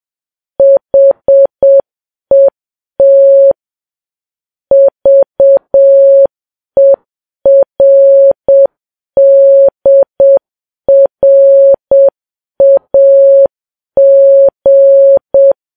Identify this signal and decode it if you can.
Morse code, heard as a keyed tone, about 7 wpm, 550 Hz. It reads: HET VERDRAG
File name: Files Web tls / Morse on9fv / morse7kps